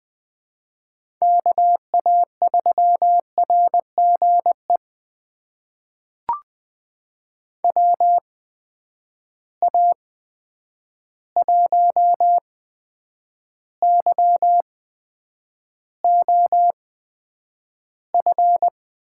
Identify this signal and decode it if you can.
Morse code, heard as a keyed tone, about 20 wpm, spaced as for 9 wpm, 700 Hz. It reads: KA3RGE WA1YOF
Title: USA Call Signs - 6 Characters - 8x Character Spacing - 20wpm